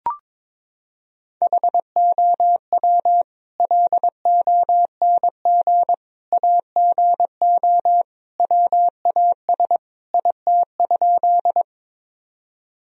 Morse code, heard as a keyed tone, about 21 wpm, 700 Hz. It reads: HOW LONG AGO WAS IT?
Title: Sentences from Top 1000 Words - Review Part 3 - 22wpm